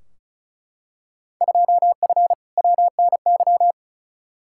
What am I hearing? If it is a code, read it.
Morse code, heard as a keyed tone, about 36 wpm, 690 Hz. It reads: 2F WNY
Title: Field Day Exchanges Encore - 35wpm